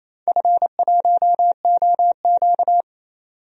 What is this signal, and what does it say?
Morse code, heard as a keyed tone, about 28 wpm, 700 Hz. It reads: F1OQ
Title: DXCC-100 Call Signs - 4 Characters - 28wpm